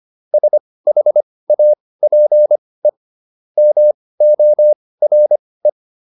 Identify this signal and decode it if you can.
Morse code, heard as a keyed tone, about 25 wpm, spaced as for 18 wpm, 600 Hz. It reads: SHAPE MORE